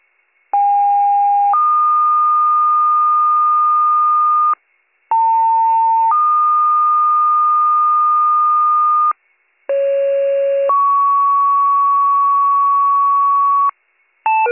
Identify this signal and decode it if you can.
Two-tone page, heard as a one-second tone, then a three-second tone, about 810 then 1220 Hz, about 900 then 1220 Hz, about 550 then 1120 Hz. It is heard from 0.5 to 13.7 s.